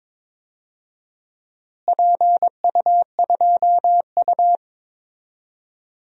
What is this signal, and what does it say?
Morse code, heard as a keyed tone, about 22 wpm, 700 Hz. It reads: PU2U
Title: DXCC-100 Call Signs - 4 Characters - Repeated 3x - 22wpm